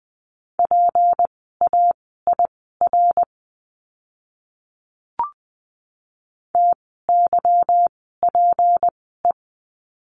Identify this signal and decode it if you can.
Morse code, heard as a keyed tone, about 20 wpm, spaced as for 14 wpm, 700 Hz. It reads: PAIR TYPE
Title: 153 Lesson 25f - Letter p - Words 20wpm